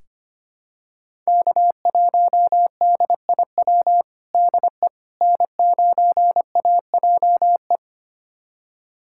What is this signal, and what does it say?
Morse code, heard as a keyed tone, about 25 wpm, 700 Hz. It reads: K1DIW DE N9AJE
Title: USA Callsign DE Callsign - 5 Characters - 25wpm